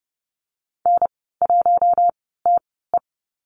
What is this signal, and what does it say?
Morse code, heard as a keyed tone, about 30 wpm, spaced as for 17 wpm, 700 Hz. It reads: N1TE
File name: Lesson 14g - Number 2 - Call Signs 30wpm